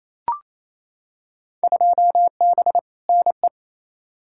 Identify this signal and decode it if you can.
Morse code, heard as a keyed tone, about 28 wpm, 700 Hz. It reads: 2B NE
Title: Field Day Exchanges Encore - 28wpm